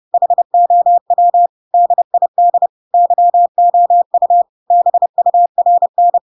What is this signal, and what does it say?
Morse code, heard as a keyed tone, about 30 wpm, 700 Hz. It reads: HOW DID YOU BURN
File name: Pinocchio / Pinocchio ch7 30 30 ik1hge